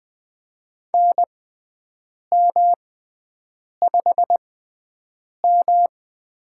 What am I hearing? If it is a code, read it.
Morse code, heard as a keyed tone, about 20 wpm, spaced as for 7 wpm, 700 Hz. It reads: NM5M
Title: USA Call Signs - 6x Character Spacing - 20wpm